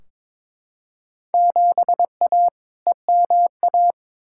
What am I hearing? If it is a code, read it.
Morse code, heard as a keyed tone, about 21 wpm, 700 Hz. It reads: 7A EMA
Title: Field Day Exchanges Encore - 22wpm